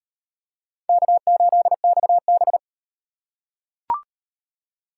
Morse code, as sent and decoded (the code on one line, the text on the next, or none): -.- ---.. -..- -...
K8XB